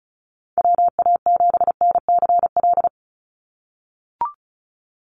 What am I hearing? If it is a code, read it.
Morse code, heard as a keyed tone, about 34 wpm, 700 Hz. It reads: WA7NCL
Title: NA SOTA Activator Callsigns 35wpm